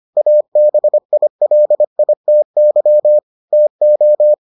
Morse code, heard as a keyed tone, about 25 wpm, 600 Hz. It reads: ABILITY TO